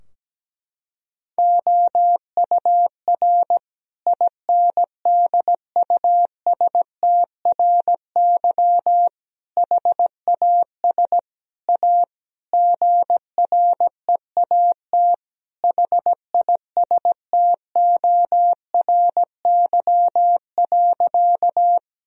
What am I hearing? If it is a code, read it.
Morse code, heard as a keyed tone, about 17 wpm, 700 Hz. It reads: OUR INDUSTRY HAS A GREAT HISTORY.
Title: Sentences from Top 1000 Words - Review Part 4 - 17wpm